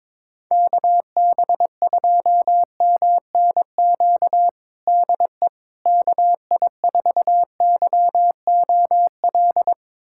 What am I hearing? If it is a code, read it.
Morse code, heard as a keyed tone, about 21 wpm, 700 Hz. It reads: KB2MNQ DE KI4YOL
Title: USA Callsign DE Callsign - 6 Characters - 22wpm